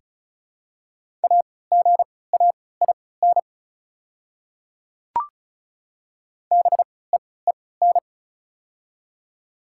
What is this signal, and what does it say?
Morse code, heard as a keyed tone, about 36 wpm, spaced as for 20 wpm, 700 Hz. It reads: AGAIN BEEN
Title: 3x Character Spacing - Top 200 Words - 35wpm